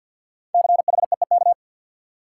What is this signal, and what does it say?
Morse code, heard as a keyed tone, about 52 wpm, 700 Hz. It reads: CHEEK
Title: Five Letter Words - 50wpm